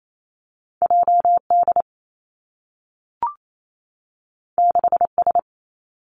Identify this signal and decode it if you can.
Morse code, heard as a keyed tone, about 28 wpm, 700 Hz. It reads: J D 6 S
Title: Two Challenging Letter-Number Combinations 28wpm